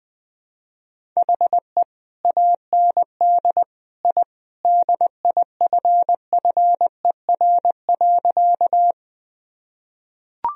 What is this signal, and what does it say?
Morse code, heard as a keyed tone, about 21 wpm, 700 Hz. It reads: HE AND I DIFFER.
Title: Sentences from Top 1000 Words - Review Part 1 - 20wpm